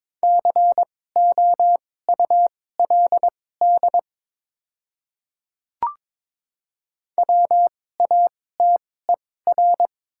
Morse code, heard as a keyed tone, about 22 wpm, spaced as for 16 wpm, 700 Hz. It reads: COULD WATER
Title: 2x Character Spacing - Top 100 Words - 22wpm